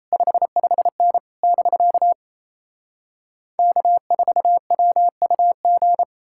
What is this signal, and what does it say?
Morse code, heard as a keyed tone, about 38 wpm, 700 Hz. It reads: 55N <BK> K4WUG